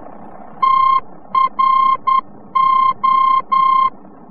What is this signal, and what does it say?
Morse code, heard as a keyed tone, about 10 wpm, 1060 Hz. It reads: TRO
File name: france / mx / TRO-VOR-F